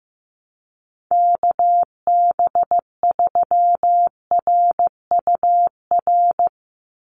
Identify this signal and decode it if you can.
Morse code, heard as a keyed tone, about 15 wpm, 700 Hz. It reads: KB3RUR
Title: USA Call Signs 15wpm